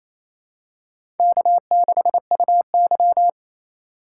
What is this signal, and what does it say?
Morse code, heard as a keyed tone, about 28 wpm, 700 Hz. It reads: K6UY
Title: USA Call Signs - 4 Characters - 3x Character Spacing - 28wpm